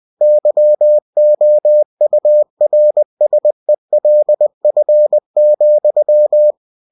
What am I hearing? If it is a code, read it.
Morse code, heard as a keyed tone, about 20 wpm, 600 Hz. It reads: YOURSELF,
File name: cw-books / winnie-the-pooh / winnie-20wpm-page-0025